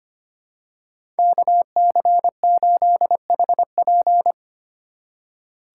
Morse code, heard as a keyed tone, about 25 wpm, 700 Hz. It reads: KC8HP
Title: USA Call Signs - 5 Characters - 3x Character Spacing - 25wpm